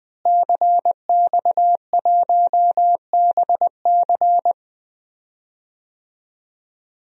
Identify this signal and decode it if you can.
Morse code, heard as a keyed tone, about 20 wpm, 700 Hz. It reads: CX1BC
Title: DXCC-100 Call Signs - 5 Characters - Repeated 2x - 20wpm